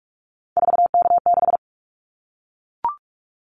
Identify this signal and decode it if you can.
Morse code, heard as a keyed tone, about 44 wpm, 700 Hz. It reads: 4K6
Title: Callsign Prefixes for Top 100 DXCC - 45wpm